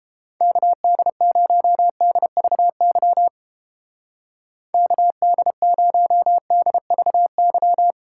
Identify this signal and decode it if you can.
Morse code, heard as a keyed tone, about 29 wpm, 700 Hz. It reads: KD0DVY KD0DVY